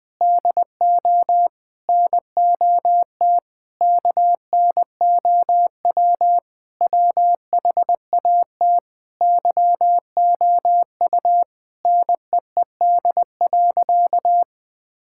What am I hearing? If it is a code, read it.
Morse code, heard as a keyed tone, about 21 wpm, 700 Hz. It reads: DO NOT KNOW WHAT YOU NEED.
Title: Sentences with 5-Word Phrases - 20wpm